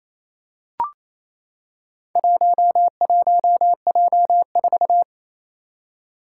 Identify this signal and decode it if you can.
Morse code, heard as a keyed tone, about 27 wpm, 700 Hz. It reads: 11J4
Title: Four Challenging Letter-Number Combinations 28wpm